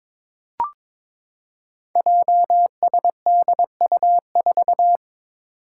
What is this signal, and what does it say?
Morse code, heard as a keyed tone, about 22 wpm, 700 Hz. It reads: JSDU4
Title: Five Challenging Letter-Number Combinations 22wpm